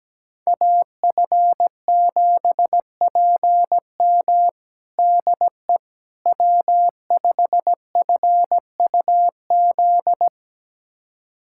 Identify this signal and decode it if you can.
Morse code, heard as a keyed tone, about 17 wpm, 700 Hz. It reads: AF7PM DE W5FUZ
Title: USA Callsign DE Callsign - 5 Characters - 17wpm